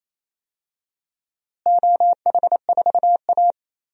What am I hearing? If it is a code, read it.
Morse code, heard as a keyed tone, about 28 wpm, 700 Hz. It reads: OH4A
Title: DXCC-100 Call Signs - 4 Characters - Repeated 3x - 28wpm